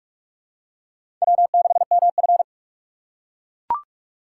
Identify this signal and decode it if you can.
Morse code, heard as a keyed tone, about 42 wpm, 700 Hz. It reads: W6MF
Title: USA Call Signs - 4 Characters - 2x Character Spacing - 45wpm